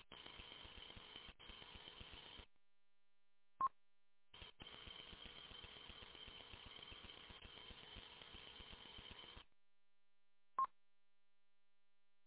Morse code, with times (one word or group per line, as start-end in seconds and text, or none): none